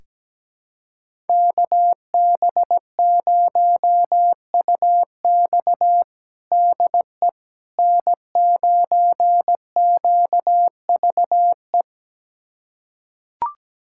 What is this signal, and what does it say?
Morse code, heard as a keyed tone, about 17 wpm, 700 Hz. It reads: KB0UX DE N9QVE
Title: USA Callsign DE Callsign 17wpm